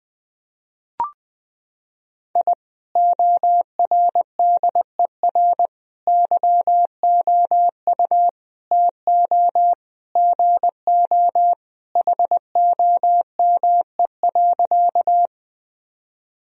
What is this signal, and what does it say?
Morse code, heard as a keyed tone, about 20 wpm, 700 Hz. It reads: I ORDER YOU TO GO HOME.
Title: Sentences from Top 1000 Words - Review Part 2 - 20wpm